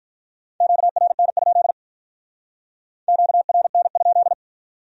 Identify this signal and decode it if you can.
Morse code, heard as a keyed tone, about 43 wpm, 700 Hz. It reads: QRN? QRN?
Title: Speed-Racing Q-Codes - 35wpm